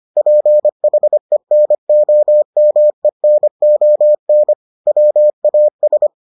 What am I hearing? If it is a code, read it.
Morse code, heard as a keyed tone, about 25 wpm, 600 Hz. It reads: PHENOMENON WAS